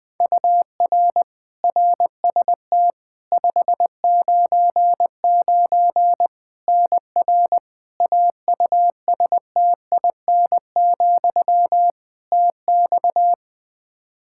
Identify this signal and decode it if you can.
Morse code, heard as a keyed tone, about 20 wpm, 700 Hz. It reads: UR RST 599 NR AUSTIN, TX
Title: Practice QSO - Exchange 1 - Beginning - 20wpm